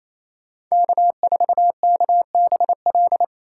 Morse code, heard as a keyed tone, about 28 wpm, 700 Hz. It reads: K4KBL
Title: K1USN SST QSO Practice - 28wpm